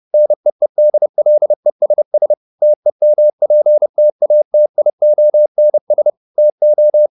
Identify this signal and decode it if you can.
Morse code, heard as a keyed tone, about 30 wpm, 600 Hz. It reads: NEEDLESS TEMPTATIONS TO
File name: the old man and the sea / the old man and the sea 1 30wpm